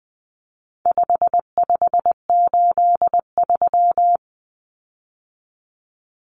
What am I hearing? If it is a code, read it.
Morse code, heard as a keyed tone, about 20 wpm, 700 Hz. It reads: 5583
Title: Common Cut Numbers - 4-Digit 20wpm